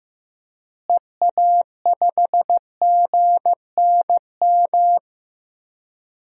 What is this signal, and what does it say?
Morse code, heard as a keyed tone, about 15 wpm, 700 Hz. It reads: EA5GNM
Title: DXCC-100 Call Signs - 6 Characters - Repeated 2x - 15wpm